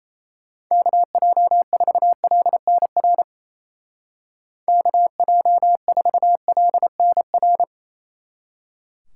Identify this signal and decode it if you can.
Morse code, heard as a keyed tone, about 33 wpm, 700 Hz. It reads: KJ4LNR KJ4LNR
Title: Speed Racing - USA 2x3 Call Signs 28wpm